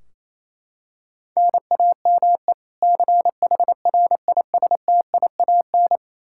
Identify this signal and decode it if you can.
Morse code, heard as a keyed tone, about 28 wpm, 700 Hz. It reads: NAME CHRISTIAN
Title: Practice QSO - Exchange 1 - Beginning - 28wpm